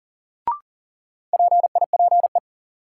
Morse code, heard as a keyed tone, about 39 wpm, 700 Hz. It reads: PIPE
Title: Sentences from Top 1500 Words - New Word Drill - 40wpm